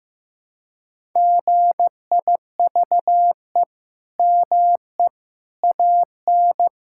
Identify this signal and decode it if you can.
Morse code, heard as a keyed tone, about 16 wpm, 700 Hz. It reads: GIVE ME AN
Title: Sentences from Top 1000 Words - Review Part 5 - 15wpm